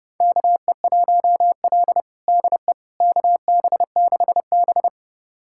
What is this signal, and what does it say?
Morse code, heard as a keyed tone, about 32 wpm, 700 Hz. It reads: KE1L DE KB6B
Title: USA Callsign DE Callsign - 4 Characters - 30wpm